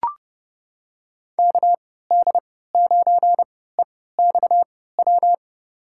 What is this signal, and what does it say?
Morse code, heard as a keyed tone, about 30 wpm, spaced as for 17 wpm, 700 Hz. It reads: KD9EXW
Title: USA Call Signs - 6 Characters - 3x Character Spacing - 30wpm